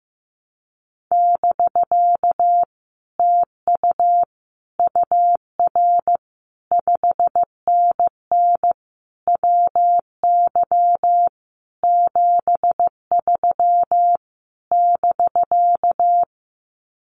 Morse code, heard as a keyed tone, about 15 wpm, 700 Hz. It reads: <BK> TU UR 5NN WY 73 <BK>
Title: Common QSO Phrases Encore 15wpm